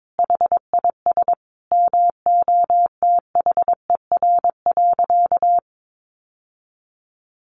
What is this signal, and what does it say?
Morse code, heard as a keyed tone, about 22 wpm, 700 Hz. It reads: HIS MOTHER.
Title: Sentences from Top 700 Words 22wpm